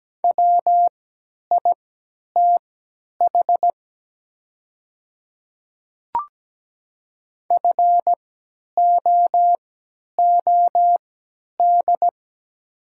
Morse code, heard as a keyed tone, about 17 wpm, spaced as for 10 wpm, 700 Hz. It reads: WITH FOOD
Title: 3x Character Spacing - Top 300 Words - 17wpm